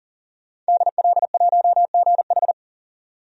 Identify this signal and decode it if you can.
Morse code, heard as a keyed tone, about 40 wpm, 700 Hz. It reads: DL1GH
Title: DXCC-100 Call Signs - 5 Characters - 40wpm